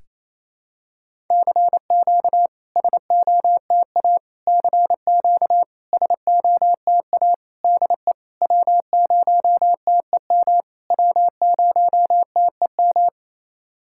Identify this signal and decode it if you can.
Morse code, heard as a keyed tone, about 28 wpm, 700 Hz. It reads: CQ SOTA CQ SOTA DE W0TEM W0TEM
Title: SOTA QSO Practice 28wpm